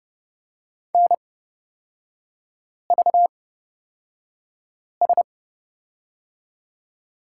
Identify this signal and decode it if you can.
Morse code, heard as a keyed tone, about 30 wpm, spaced as for 5 wpm, 700 Hz. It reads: NVS
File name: Lesson 38e - Letter q - Three Characters 30wpm